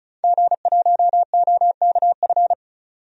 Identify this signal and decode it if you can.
Morse code, heard as a keyed tone, about 34 wpm, 700 Hz. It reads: G1OKF
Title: DXCC-100 Call Signs - 5 Characters - 35wpm